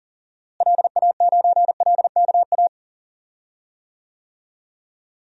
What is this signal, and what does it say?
Morse code, heard as a keyed tone, about 40 wpm, 700 Hz. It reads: LA9LKA
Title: DXCC-100 Call Signs - 6 Characters - Repeated 2x - 40wpm